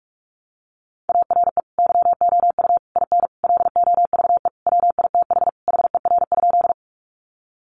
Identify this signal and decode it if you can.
Morse code, heard as a keyed tone, about 44 wpm, 700 Hz. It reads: ARE YOU IN LOVE WITH HER?
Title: Sentences from Top 400 Words 45wpm